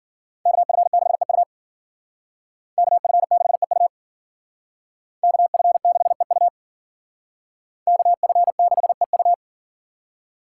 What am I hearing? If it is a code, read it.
Morse code, heard as a keyed tone, about 51 wpm, 700 Hz. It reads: KF6EU KF6EU KF6EU KF6EU